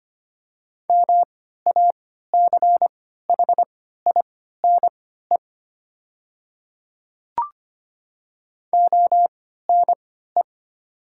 Morse code, heard as a keyed tone, about 25 wpm, spaced as for 14 wpm, 700 Hz. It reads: MACHINE ONE
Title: 3x Character Spacing - Top 500 Words - 25wpm